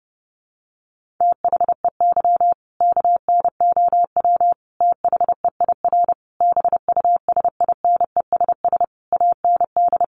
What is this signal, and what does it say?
Morse code, heard as a keyed tone, about 30 wpm, 700 Hz. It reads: THEY KNOW THEIR BUSINESS AND
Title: Sentences from Top 1200 Words 30wpm